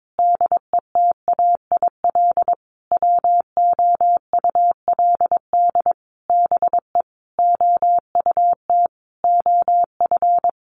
Morse code, heard as a keyed tone, about 22 wpm, 700 Hz. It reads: DETAIL WOULD BE OUT OF
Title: Sentences from Top 1500 Words 22wpm